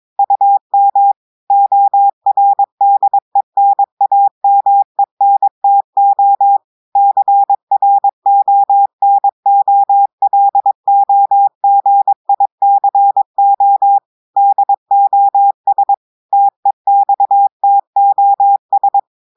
Morse code, heard as a keyed tone, about 22 wpm, 800 Hz. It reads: UM ORDENAMENTO CRONOLOGICO DOS TEXTOS
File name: audios / portugais / portugais texte en clair 1 vitesse 22 mots minutes